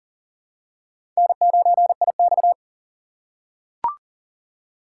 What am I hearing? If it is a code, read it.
Morse code, heard as a keyed tone, about 40 wpm, 700 Hz. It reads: N9IX
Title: USA Call Signs - 4 Characters - 2x Character Spacing - 40wpm